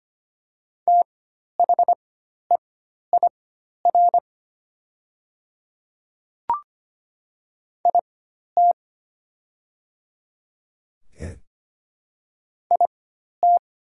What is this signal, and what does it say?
Morse code, heard as a keyed tone, about 25 wpm, spaced as for 12 wpm, 700 Hz. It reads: THEIR IT IT
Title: 4x Character Spacing - Top 100 Words - 25wpm